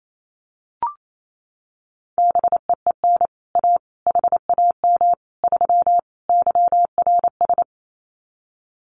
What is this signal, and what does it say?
Morse code, heard as a keyed tone, about 28 wpm, 700 Hz. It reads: BEEN A HAM 3 YRS
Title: Common QSO Phrases Encore 28wpm